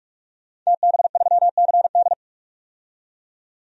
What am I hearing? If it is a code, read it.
Morse code, heard as a keyed tone, about 46 wpm, 700 Hz. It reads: TB3CD
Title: DXCC-100 Call Signs - 5 Characters - 45wpm